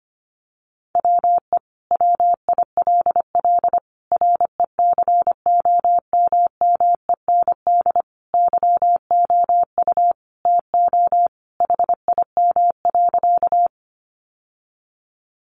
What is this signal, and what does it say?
Morse code, heard as a keyed tone, about 25 wpm, 700 Hz. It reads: WE WILL RECOMMEND YOU TO HIM.
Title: Sentences from Top 1200 Words 25wpm